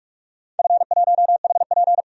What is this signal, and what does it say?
Morse code, heard as a keyed tone, about 46 wpm, 700 Hz. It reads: F1HP
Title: DXCC-100 Call Signs - 4 Characters - 45wpm